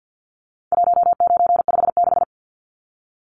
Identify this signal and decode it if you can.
Morse code, heard as a keyed tone, about 50 wpm, 700 Hz. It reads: 1956
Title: Numbers - 4-Digit 50wpm